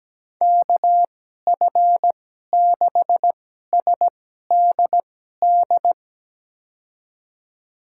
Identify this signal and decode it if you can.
Morse code, heard as a keyed tone, about 17 wpm, spaced as for 12 wpm, 700 Hz. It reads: KF6SDD
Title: USA Call Signs - 2x Character Spacing - 17wpm